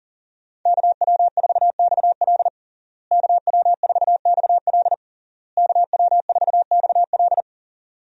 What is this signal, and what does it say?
Morse code, heard as a keyed tone, about 40 wpm, 700 Hz. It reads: KW4XL KW4XL KW4XL
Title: USA Call Signs - 5 Characters - Repeated 3x - 40wpm